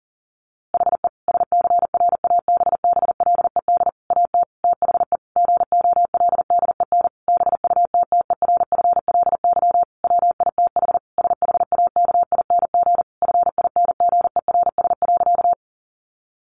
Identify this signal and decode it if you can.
Morse code, heard as a keyed tone, about 40 wpm, 700 Hz. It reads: HE SCRABBLED AT THE GOLDEN BUTTERFLY WITH SHAKING FINGERS.
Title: A Sound Of Thunder 40wpm